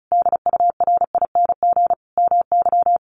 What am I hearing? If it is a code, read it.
Morse code, heard as a keyed tone, about 34 wpm, 700 Hz. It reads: DURING MY
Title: Sentences from Top 1300 Words 35wpm